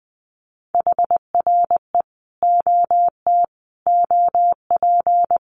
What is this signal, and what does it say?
Morse code, heard as a keyed tone, about 19 wpm, 700 Hz. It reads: HRE OT OP
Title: Common QSO Phrases 20wpm